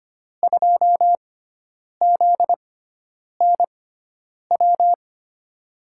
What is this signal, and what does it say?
Morse code, heard as a keyed tone, about 25 wpm, spaced as for 9 wpm, 700 Hz. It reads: 2ZNW
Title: USA Call Signs - 6 Characters - 6x Character Spacing - 25wpm